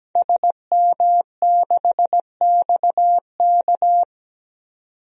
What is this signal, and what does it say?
Morse code, heard as a keyed tone, about 17 wpm, 700 Hz. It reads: SM6XK
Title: DXCC-100 Call Signs - 5 Characters - 17wpm